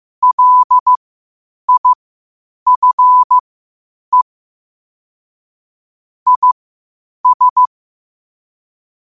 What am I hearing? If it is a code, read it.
Morse code, heard as a keyed tone, about 15 wpm, spaced as for 8 wpm, 1000 Hz. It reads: LIFE IS
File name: qotdfiles / 8wpm